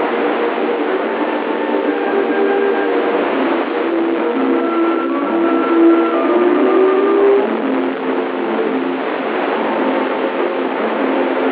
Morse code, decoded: none